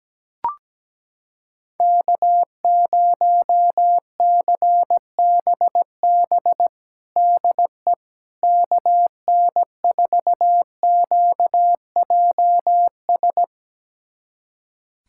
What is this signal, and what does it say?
Morse code, heard as a keyed tone, about 17 wpm, 700 Hz. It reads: K0CBB DE KN4QJS
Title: USA Callsign DE Callsign 17wpm